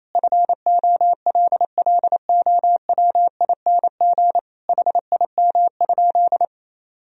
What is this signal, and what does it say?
Morse code, heard as a keyed tone, about 27 wpm, 700 Hz. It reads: FOLLOWING HIM?
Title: Sentences from Top 1000 Words - Review Part 4 - 28wpm